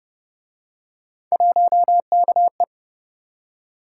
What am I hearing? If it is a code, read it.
Morse code, heard as a keyed tone, about 29 wpm, 700 Hz. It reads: 1KE